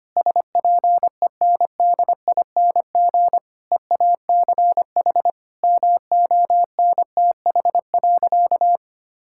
Morse code, heard as a keyed tone, about 25 wpm, 700 Hz. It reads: SPENDING EACH MONTH.
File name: practice / 25wpm / Difficult Sentences with Hints - 25wpm